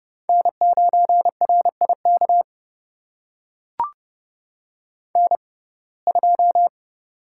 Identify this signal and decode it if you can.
Morse code, heard as a keyed tone, about 30 wpm, spaced as for 22 wpm, 700 Hz. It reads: N9RIK N2
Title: USA Call Signs - 5 Characters - 6x Character Spacing - 30wpm